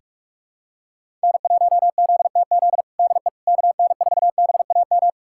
Morse code, heard as a keyed tone, about 46 wpm, 700 Hz. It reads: N1ZTZ DE KN4BAM
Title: USA Callsign DE Callsign 45wpm